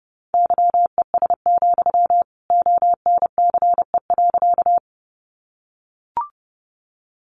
Morse code, heard as a keyed tone, about 30 wpm, 700 Hz. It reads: YES, ONCE.
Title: Sentences from Top 500 Words 30wpm